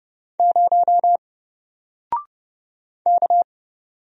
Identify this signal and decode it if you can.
Morse code, heard as a keyed tone, about 28 wpm, 700 Hz. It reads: 0 K
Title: Single Character (Letter:Number:Punctuation:Prosign) 30wpm